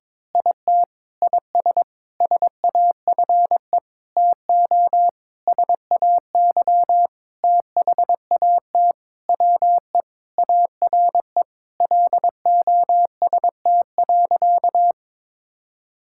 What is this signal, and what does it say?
Morse code, heard as a keyed tone, about 22 wpm, 700 Hz. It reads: IT IS SAFE TO SAY THAT WE ARE LOST.
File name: practice / 22wpm / Sentences with 7-Word Phrases - 22wpm